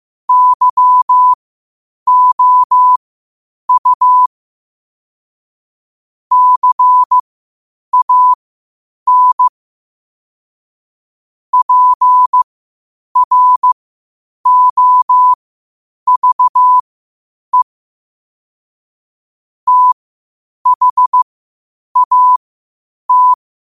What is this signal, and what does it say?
Morse code, heard as a keyed tone, about 15 wpm, spaced as for 8 wpm, 1000 Hz. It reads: YOU CAN PROVE THAT
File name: qotdfiles / 8wpm